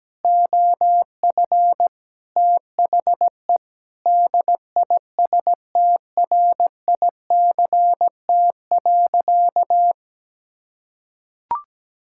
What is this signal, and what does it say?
Morse code, heard as a keyed tone, about 17 wpm, 700 Hz. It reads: OF THE DISTRICT.
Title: Sentences from Top 1600 Words 17wpm